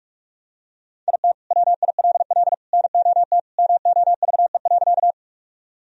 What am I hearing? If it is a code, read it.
Morse code, heard as a keyed tone, about 46 wpm, 700 Hz. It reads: IT WILL NOT MOVE.
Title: Sentences from Top 1000 Words - Review Part 1 - 45wpm